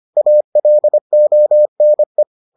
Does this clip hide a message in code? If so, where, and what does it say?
Yes: Morse code from 0.2 to 2.2 s, heard as a keyed tone, about 25 wpm, 600 Hz. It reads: ALONE